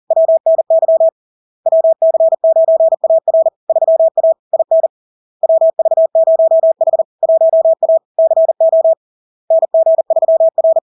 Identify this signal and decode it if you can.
Morse code, heard as a keyed tone, about 40 wpm, 650 Hz. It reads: WNY WC9AR 3A IN WV0H 1A CO NG3R
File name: LARK-FD-2022-Log-in-CW-40wpm-650-Hz